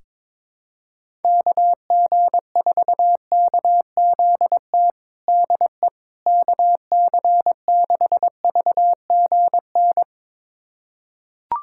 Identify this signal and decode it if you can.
Morse code, heard as a keyed tone, about 22 wpm, 700 Hz. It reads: KG4KZT DE KC6VGN
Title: USA Callsign DE Callsign 22wpm